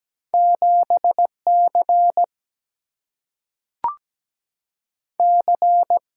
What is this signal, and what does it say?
Morse code, heard as a keyed tone, about 17 wpm, 700 Hz. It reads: 7C C
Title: Single-Letter Single-Number 17wpm